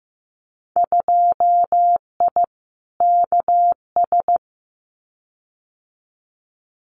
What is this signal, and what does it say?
Morse code, heard as a keyed tone, about 15 wpm, 700 Hz. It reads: 2I KS
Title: Winter Field Day Exchanges Encore 15wpm